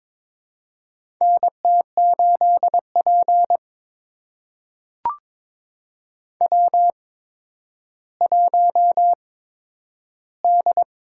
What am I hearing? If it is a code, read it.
Morse code, heard as a keyed tone, about 23 wpm, spaced as for 11 wpm, 700 Hz. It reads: NT8P W1D
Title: USA Call Signs - 8x Character Spacing - 22wpm